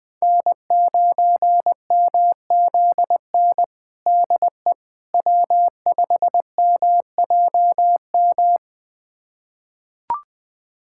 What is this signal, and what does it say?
Morse code, heard as a keyed tone, about 20 wpm, 700 Hz. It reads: N9MZN DE W5MJM
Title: USA Callsign DE Callsign - 5 Characters - 20wpm